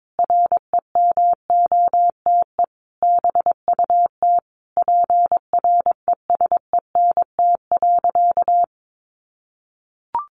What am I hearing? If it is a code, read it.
Morse code, heard as a keyed tone, about 22 wpm, 700 Hz. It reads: REMOTE BUT PRESENT.
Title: Sentences from Top 1300 Words 22wpm